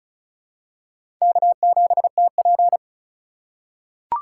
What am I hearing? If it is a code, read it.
Morse code, heard as a keyed tone, about 34 wpm, 700 Hz. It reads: K7TP
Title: USA Call Signs - 4 Characters - 3x Character Spacing - 35wpm